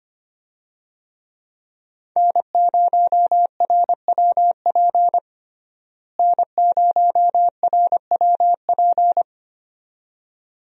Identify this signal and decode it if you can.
Morse code, heard as a keyed tone, about 25 wpm, 700 Hz. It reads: N0RWP N0RWP
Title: USA Call Signs - 5 Characters - Repeated 2x - 25wpm